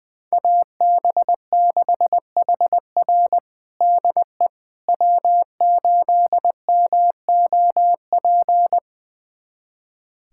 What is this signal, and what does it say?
Morse code, heard as a keyed tone, about 21 wpm, 700 Hz. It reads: AB6HR DE W8MOP
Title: USA Callsign DE Callsign - 5 Characters - 20wpm